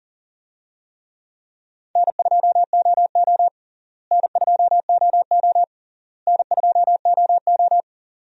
Morse code, heard as a keyed tone, about 40 wpm, 700 Hz. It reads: N2OO N2OO N2OO
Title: USA Call Signs - 4 Characters - Repeated 3x - 40wpm